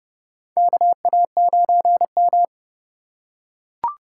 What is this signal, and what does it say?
Morse code, heard as a keyed tone, about 28 wpm, 690 Hz. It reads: KA9M
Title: USA Call Signs - 4 Characters - 2x Character Spacing - 30wpm